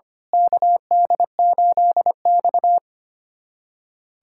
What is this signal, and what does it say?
Morse code, heard as a keyed tone, about 25 wpm, 700 Hz. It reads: KD8X
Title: USA Callsign DE Callsign - 4 Characters - 25wpm